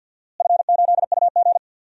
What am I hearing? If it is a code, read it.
Morse code, heard as a keyed tone, about 52 wpm, 700 Hz. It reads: F8UG
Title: DXCC-100 Call Signs - 4 Characters - Repeated 2x - 50wpm